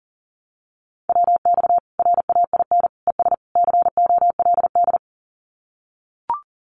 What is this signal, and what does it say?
Morse code, heard as a keyed tone, about 40 wpm, 700 Hz. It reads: WX RAIN ES COLD
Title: Common QSO Phrases 40wpm